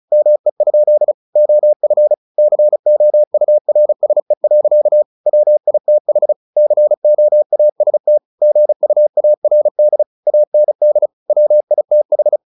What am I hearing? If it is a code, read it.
Morse code, heard as a keyed tone, about 35 wpm, 600 Hz. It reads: ME? OF COURSE. WITH COAST GUARD AND WITH
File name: the old man and the sea 2 35wpm